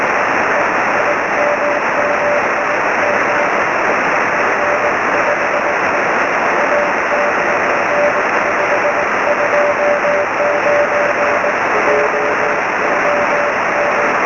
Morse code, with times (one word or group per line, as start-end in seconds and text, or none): none